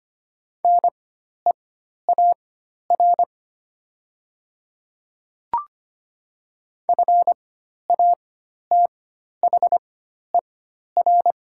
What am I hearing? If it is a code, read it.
Morse code, heard as a keyed tone, about 25 wpm, spaced as for 12 wpm, 700 Hz. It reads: NEAR FATHER
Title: 4x Character Spacing - Top 500 Words - 25wpm